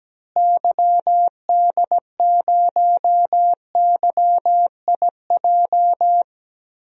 Morse code, heard as a keyed tone, about 17 wpm, 700 Hz. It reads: YD0YIJ
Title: DXCC-100 Call Signs - 6 Characters - Repeated 3x - 17wpm